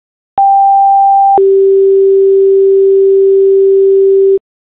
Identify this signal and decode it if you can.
Two-tone page, heard as a one-second tone, then a three-second tone, about 790 then 390 Hz. It is heard from 0.4 to 4.4 s.